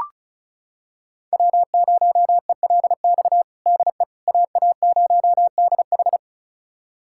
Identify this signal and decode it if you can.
Morse code, heard as a keyed tone, about 36 wpm, 700 Hz. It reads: W0ELX DE AA0DH
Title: USA Callsign DE Callsign - 5 Characters - 35wpm